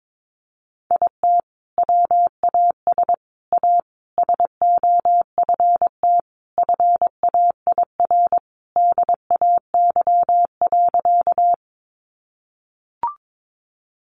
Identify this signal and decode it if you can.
Morse code, heard as a keyed tone, about 22 wpm, 700 Hz. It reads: IT WAS A SOFT FAIR DAY.
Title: Sentences from Top 1500 Words 22wpm